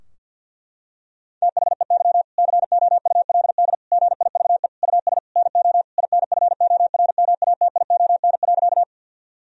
Morse code, heard as a keyed tone, about 50 wpm, 700 Hz. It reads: THEY COULD GIVE US NO INFORMATION.
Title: Sentences from Top 1000 Words - Review Part 3 - 50wpm